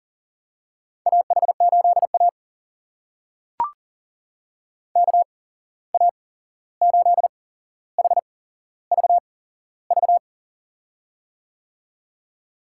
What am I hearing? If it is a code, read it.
Morse code, heard as a keyed tone, about 40 wpm, spaced as for 15 wpm, 700 Hz. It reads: AH8A KA8HVV
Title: USA Call Signs - 8x Character Spacing - 40wpm